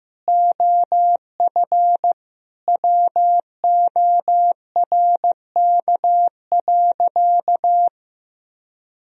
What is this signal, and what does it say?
Morse code, heard as a keyed tone, about 16 wpm, 700 Hz. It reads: OF WORK.
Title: Sentences from Top 1000 Words - Review Part 4 - 15wpm